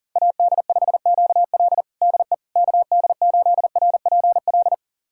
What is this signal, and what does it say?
Morse code, heard as a keyed tone, about 40 wpm, 700 Hz. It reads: AD5QL DE KD8RPL
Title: USA Callsign DE Callsign 40wpm